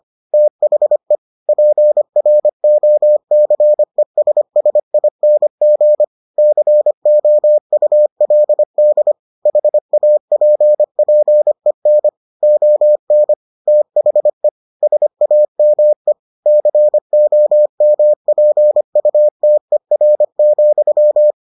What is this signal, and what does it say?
Morse code, heard as a keyed tone, about 25 wpm, 600 Hz. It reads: THE PROCESSING COULD HAPPEN ON THE SAME COMPUTER,